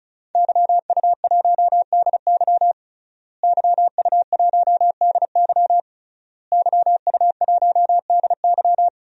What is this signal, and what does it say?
Morse code, heard as a keyed tone, about 36 wpm, 700 Hz. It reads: YU1DY YU1DY YU1DY
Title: DXCC-100 Call Signs - 5 Characters - Repeated 3x - 35wpm